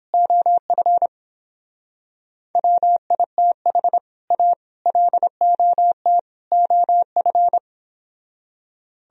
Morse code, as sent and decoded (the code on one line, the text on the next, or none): --- ..-. / .-- .. - .... / .- / .-.. --- - / --- ..-.
OF WITH A LOT OF